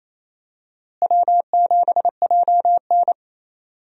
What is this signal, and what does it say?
Morse code, heard as a keyed tone, about 27 wpm, 700 Hz. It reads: W7JN
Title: USA Call Signs - 4 Characters - Repeated 3x - 28wpm